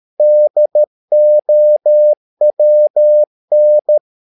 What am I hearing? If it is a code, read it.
Morse code, heard as a keyed tone, about 13 wpm, 600 Hz. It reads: DOWN
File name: mp3 files / Lexiphiles - 13wpm